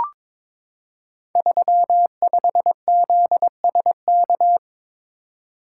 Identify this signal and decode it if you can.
Morse code, heard as a keyed tone, about 23 wpm, 700 Hz. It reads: 35ZSK
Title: Code Groups - 5 Letter:Numbers - 22wpm